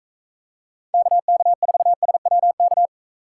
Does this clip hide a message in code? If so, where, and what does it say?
Yes: Morse code from 0.9 to 2.9 s, heard as a keyed tone, about 40 wpm, 700 Hz. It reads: KK4SWK